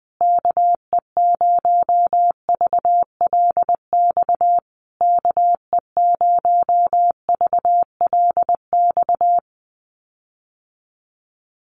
Morse code, heard as a keyed tone, about 20 wpm, 700 Hz. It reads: KE0VLX KE0VLX
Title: USA Call Signs Repeated 3x 20wpm